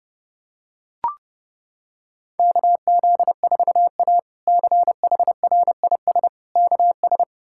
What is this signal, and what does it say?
Morse code, heard as a keyed tone, about 32 wpm, 700 Hz. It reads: KZ4A CHRIS KS
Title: K1USN SST QSO Practice - 30wpm